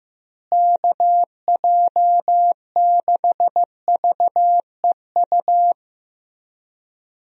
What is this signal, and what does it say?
Morse code, heard as a keyed tone, about 15 wpm, 700 Hz. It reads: KJ6VEU
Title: USA Call Signs - 6 Characters - Repeated 3x - 15wpm